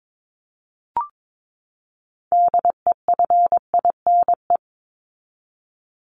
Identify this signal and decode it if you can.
Morse code, heard as a keyed tone, about 22 wpm, 700 Hz. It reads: DEFINE
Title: ED Suffix - Top 100 Words with Base Words 22wpm